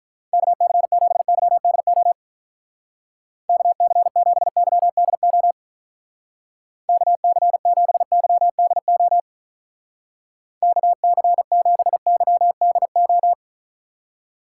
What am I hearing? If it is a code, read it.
Morse code, heard as a keyed tone, about 45 wpm, 700 Hz. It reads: KC7YDO KC7YDO KC7YDO KC7YDO